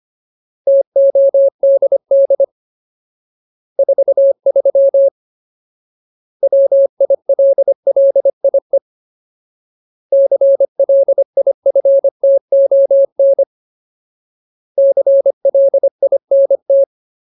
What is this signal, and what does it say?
Morse code, heard as a keyed tone, about 25 wpm, 550 Hz. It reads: TODD 43 WILLIE CLIFTON CLINT
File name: Ultimate Word List / 25WPM Ultimate